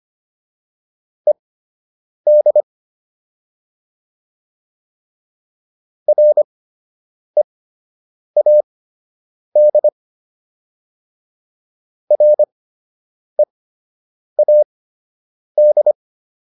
Morse code, heard as a keyed tone, about 25 wpm, spaced as for 8 wpm, 600 Hz. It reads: ED READ READ
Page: fundamental-practice-files